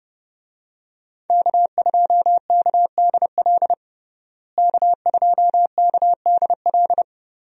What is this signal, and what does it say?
Morse code, heard as a keyed tone, about 30 wpm, 700 Hz. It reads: K2KDL K2KDL
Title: USA Call Signs - 5 Characters - Repeated 2x - 30wpm